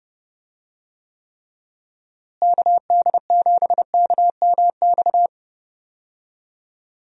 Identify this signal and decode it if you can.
Morse code, heard as a keyed tone, about 30 wpm, 700 Hz. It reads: KD7KMX
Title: USA Call Signs - 6 Characters - Repeated 2x - 30wpm